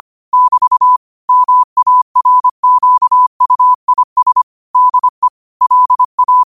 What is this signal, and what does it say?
Morse code, heard as a keyed tone, about 25 wpm, 1000 Hz. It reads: <BT> MARQUIS DE LA
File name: qotdfiles / 25wpm